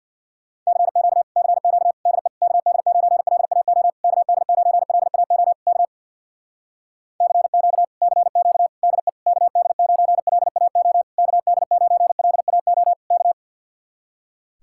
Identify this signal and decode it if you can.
Morse code, heard as a keyed tone, about 65 wpm, 700 Hz. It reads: CQ CQ DE KD9LAO KD9LAO K CQ CQ DE KD9LAO KD9LAO K